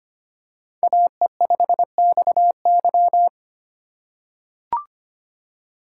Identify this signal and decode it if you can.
Morse code, heard as a keyed tone, about 25 wpm, 700 Hz. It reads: AE5XY
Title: USA Call Signs - 6x Character Spacing - 25wpm